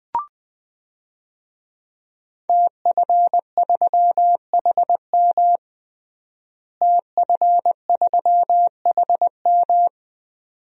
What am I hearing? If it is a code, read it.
Morse code, heard as a keyed tone, about 20 wpm, 700 Hz. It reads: TF3HM TF3HM
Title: DXCC-100 Call Signs - 5 Characters - Repeated 2x - 20wpm